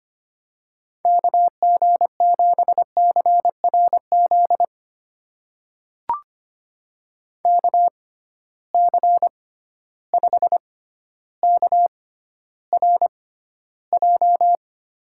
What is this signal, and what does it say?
Morse code, heard as a keyed tone, about 25 wpm, spaced as for 13 wpm, 700 Hz. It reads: KG7CRZ KC5KRJ
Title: USA Call Signs - 6 Characters - 6x Character Spacing - 25wpm